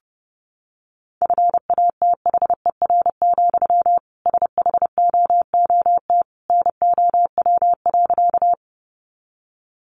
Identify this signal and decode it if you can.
Morse code, heard as a keyed tone, about 29 wpm, 700 Hz. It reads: FATHER, SHOOT NOW.
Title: Sentences from Top 1200 Words 30wpm